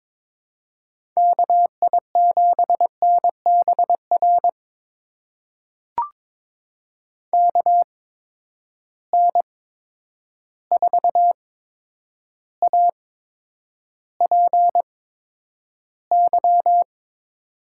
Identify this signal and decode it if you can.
Morse code, heard as a keyed tone, about 22 wpm, spaced as for 9 wpm, 700 Hz. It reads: KI7NBR KN4APY
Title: USA Call Signs - 6 Characters - 8x Character Spacing - 22wpm